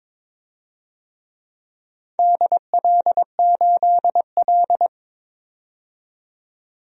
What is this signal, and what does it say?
Morse code, heard as a keyed tone, about 22 wpm, 700 Hz. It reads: DL8L
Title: DXCC-100 Call Signs - 4 Characters - 22wpm